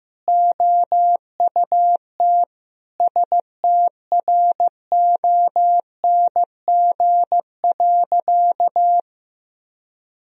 Morse code, heard as a keyed tone, about 15 wpm, 700 Hz. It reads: OUT STRONG.
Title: Sentences from Top 1000 Words - Review Part 3 - 15wpm